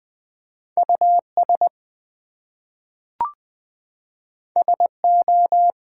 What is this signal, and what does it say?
Morse code, heard as a keyed tone, about 20 wpm, 700 Hz. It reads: US SO
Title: Two Letter Words - 20wpm